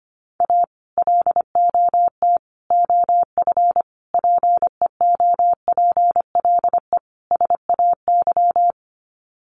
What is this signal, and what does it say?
Morse code, heard as a keyed tone, about 25 wpm, 700 Hz. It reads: A LOT OF PEOPLE SAY
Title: Sets of 5 Words Encore2 25wpm